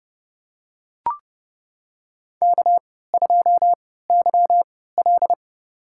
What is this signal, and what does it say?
Morse code, heard as a keyed tone, about 32 wpm, spaced as for 17 wpm, 700 Hz. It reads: K2YL
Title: USA Call Signs - 4 Characters - 3x Character Spacing - 30wpm